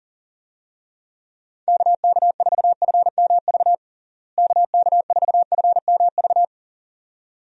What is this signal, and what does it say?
Morse code, heard as a keyed tone, about 40 wpm, 700 Hz. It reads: KK4FMV KK4FMV
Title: USA Call Signs - 6 Characters - Repeated 2x - 40wpm